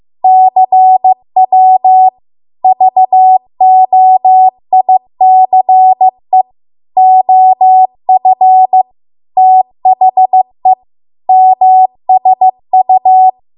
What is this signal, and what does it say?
Morse code, heard as a keyed tone, about 15 wpm, 750 Hz. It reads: CW VOICE OF THE MSU
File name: Archive / 15 WPM / 15WPM